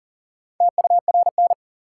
Morse code, heard as a keyed tone, about 40 wpm, 700 Hz. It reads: TURN
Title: Sets of 4 Words - New Word Drill - 40wpm